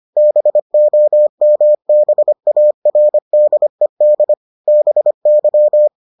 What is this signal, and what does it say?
Morse code, heard as a keyed tone, about 25 wpm, 600 Hz. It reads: BOMBARDED BY